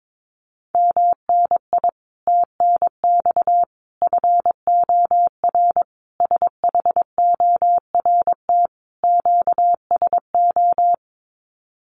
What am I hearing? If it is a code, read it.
Morse code, heard as a keyed tone, about 22 wpm, 700 Hz. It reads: MNI TNX FOR SHORT QSO
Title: Common QSO Phrases 22wpm